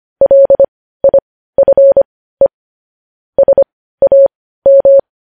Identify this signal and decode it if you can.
Morse code, heard as a keyed tone, about 25 wpm, spaced as for 15 wpm, 550 Hz. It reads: LIFE SAM
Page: news headlines in morse code at 15 wpm